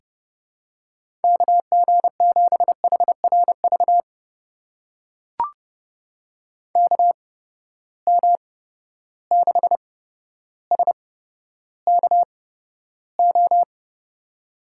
Morse code, heard as a keyed tone, about 32 wpm, spaced as for 13 wpm, 700 Hz. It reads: KG7HRV KM6SKO
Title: USA Call Signs - 6 Characters - 8x Character Spacing - 30wpm